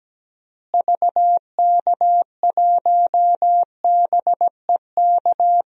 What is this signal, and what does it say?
Morse code, heard as a keyed tone, about 17 wpm, 700 Hz. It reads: VK1BEK
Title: DXCC-100 Call Signs - 6 Characters - Repeated 3x - 17wpm